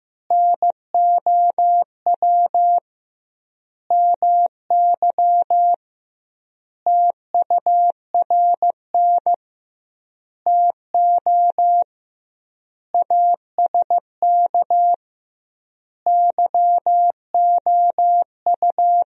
Morse code, heard as a keyed tone, about 15 wpm, 700 Hz. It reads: NOW MY TURN TO ASK YOU
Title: 2x Word-Spacing - Sentences from Top 200 Words - 15wpm